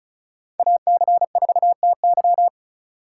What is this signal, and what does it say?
Morse code, heard as a keyed tone, about 34 wpm, 700 Hz. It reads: AC4TY
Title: USA Call Signs - 6x Character Spacing - 35wpm